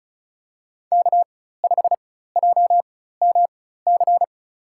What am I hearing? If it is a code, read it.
Morse code, heard as a keyed tone, about 34 wpm, spaced as for 16 wpm, 700 Hz. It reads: K5JMC
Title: USA Call Signs - 5 Characters - 4x Character Spacing - 35wpm